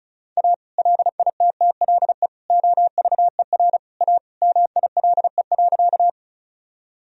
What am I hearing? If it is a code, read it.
Morse code, heard as a keyed tone, about 36 wpm, 700 Hz. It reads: A LITTLE OVER A MILE.
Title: Sentences from Top 1000 Words - Review Part 1 - 35wpm